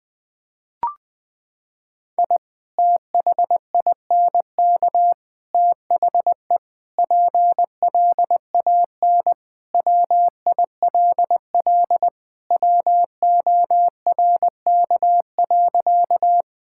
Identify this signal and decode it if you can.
Morse code, heard as a keyed tone, about 20 wpm, 700 Hz. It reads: I THINK THE PLAN WILL WORK.
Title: Sentences with 3-Word Phrases - 20wpm